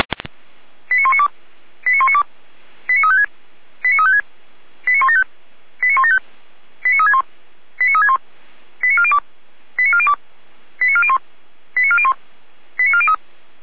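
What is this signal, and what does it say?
Five-tone selective call (ZVEI), heard as quick runs of five tones, tones of about 70 ms. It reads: 89192 89192 89356 89356 89175 89175 89361 89361 89402 89402 89401 89401 89403
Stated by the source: ZVEI five-tone selective call 89391, received by SDR, on the Feuerwehr Orlamünde audio server.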